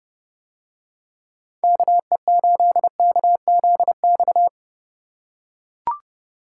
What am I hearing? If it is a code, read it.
Morse code, heard as a keyed tone, about 31 wpm, 700 Hz. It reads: KE8KZX